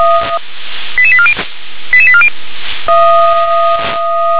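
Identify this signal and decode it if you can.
Five-tone selective call (ZVEI), heard as quick runs of five tones, tones of about 70 ms. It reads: 88840 88840
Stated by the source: ZVEI five-tone selective call received by SDR.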